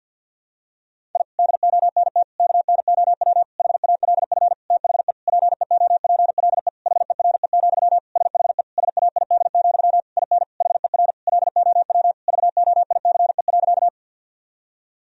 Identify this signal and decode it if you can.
Morse code, heard as a keyed tone, about 52 wpm, 700 Hz. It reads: I DONT KNOW HALF THE PEOPLE HERE, SHE SAID, IN HER LOW VOICE.
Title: Sentences from Top 1000 Words - Review Part 1 - 50wpm